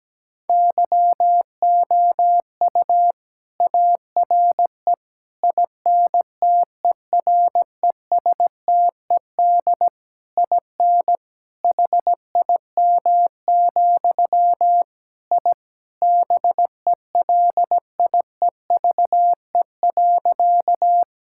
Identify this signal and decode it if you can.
Morse code, heard as a keyed tone, about 17 wpm, 700 Hz. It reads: YOU ARE INTERESTED IN HIM, I BELIEVE.
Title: Sentences from Top 1000 Words - Review Part 6 - 17wpm